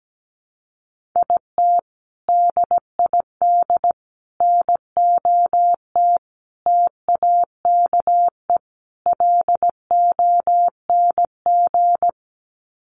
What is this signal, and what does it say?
Morse code, heard as a keyed tone, about 17 wpm, 700 Hz. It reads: IT DID NOT TAKE LONG
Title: Sets of 5 Words Encore2 17wpm